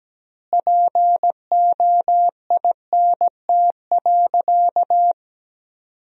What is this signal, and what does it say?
Morse code, heard as a keyed tone, about 17 wpm, 700 Hz. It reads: POINT.
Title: 3x Word-Spacing - Sentences from Top 200 Words - 17wpm